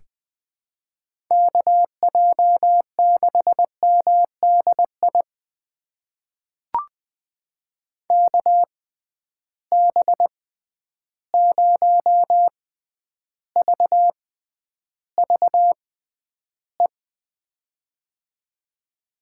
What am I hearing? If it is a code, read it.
Morse code, heard as a keyed tone, about 20 wpm, spaced as for 10 wpm, 700 Hz. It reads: KJ6MDI KB0VVE
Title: USA Call Signs - 6 Characters - 6x Character Spacing - 20wpm